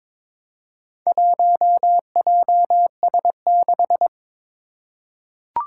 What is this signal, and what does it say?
Morse code, heard as a keyed tone, about 22 wpm, 700 Hz. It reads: 1JS6
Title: Four Challenging Letter-Number Combinations 22wpm